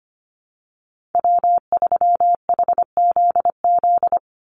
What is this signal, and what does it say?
Morse code, heard as a keyed tone, about 25 wpm, 700 Hz. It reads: W3HZZ
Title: Call Signs Repeated 3x - 25wpm